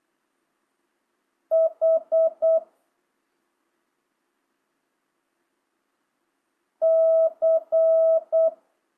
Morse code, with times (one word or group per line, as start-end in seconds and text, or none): 1.5-2.6 H
6.8-8.5 C